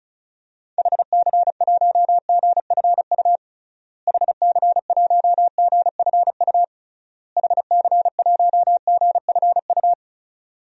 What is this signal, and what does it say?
Morse code, heard as a keyed tone, about 36 wpm, 700 Hz. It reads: HC1GFU HC1GFU HC1GFU
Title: DXCC-100 Call Signs - 6 Characters - Repeated 3x - 35wpm